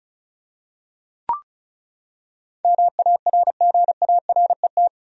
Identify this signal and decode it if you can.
Morse code, heard as a keyed tone, about 36 wpm, 700 Hz. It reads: MARGARET
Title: USA Common Names - 35wpm